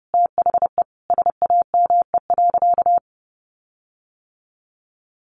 Morse code, heard as a keyed tone, about 30 wpm, 700 Hz. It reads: THE SAME.
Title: Sentences from Top 1300 Words 30wpm